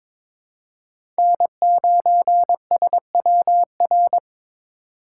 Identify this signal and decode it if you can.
Morse code, heard as a keyed tone, about 22 wpm, 700 Hz. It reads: N9SWR
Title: USA Call Signs - 5 Characters - 2x Character Spacing - 22wpm